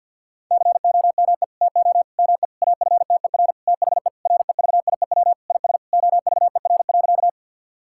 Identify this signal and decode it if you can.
Morse code, heard as a keyed tone, about 52 wpm, 700 Hz. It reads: COME TO ME AFTER THE REVIEW IS OVER.
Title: Sentences from Top 1000 Words - Review Part 6 - 50wpm